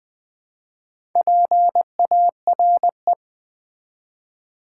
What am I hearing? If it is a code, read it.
Morse code, heard as a keyed tone, about 20 wpm, 700 Hz. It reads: PARE
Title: Build A Word - 2 Letters At A Time - 8-Letter Words - 20wpm